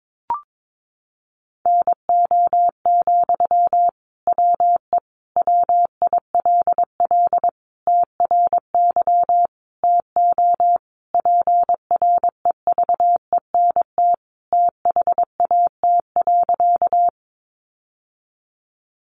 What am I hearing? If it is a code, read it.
Morse code, heard as a keyed tone, about 22 wpm, 700 Hz. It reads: NO, WE WILL TRY TO PREVENT THAT.
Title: Sentences from Top 1100 Words 22wpm